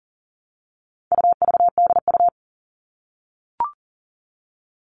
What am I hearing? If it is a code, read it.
Morse code, heard as a keyed tone, about 40 wpm, 700 Hz. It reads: UVDU
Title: All Cut Numbers - 4-Digit 40wpm